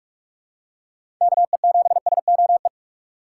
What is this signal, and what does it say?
Morse code, heard as a keyed tone, about 42 wpm, 700 Hz. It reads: KE7SOE
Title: USA Call Signs - 6 Characters - 2x Character Spacing - 45wpm